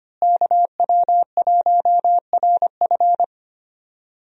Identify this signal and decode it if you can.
Morse code, heard as a keyed tone, about 25 wpm, 700 Hz. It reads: KW1RF
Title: USA Call Signs - 5 Characters - Repeated 2x - 25wpm